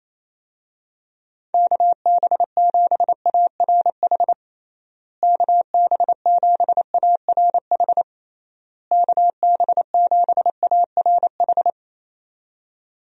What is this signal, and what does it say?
Morse code, heard as a keyed tone, about 28 wpm, 700 Hz. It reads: KB7ARH KB7ARH KB7ARH
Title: USA Call Signs - 6 Characters - Repeated 3x - 28wpm